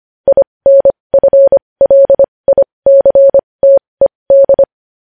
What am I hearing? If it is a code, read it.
Morse code, heard as a keyed tone, about 25 wpm, spaced as for 20 wpm, 550 Hz. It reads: INFLICTED
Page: News headlines in Morse code at 20 wpm